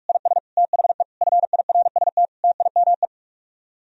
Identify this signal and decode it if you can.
Morse code, heard as a keyed tone, about 50 wpm, 700 Hz. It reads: IS THE FIRST TIME